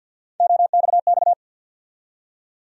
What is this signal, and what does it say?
Morse code, heard as a keyed tone, about 52 wpm, 700 Hz. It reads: OXX
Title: USA Call Sign Postfixes - 3 Letters 50wpm